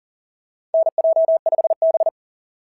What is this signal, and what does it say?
Morse code, heard as a keyed tone, about 40 wpm, 660 Hz. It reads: NJ5B
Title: SOTA QSO Practice 40wpm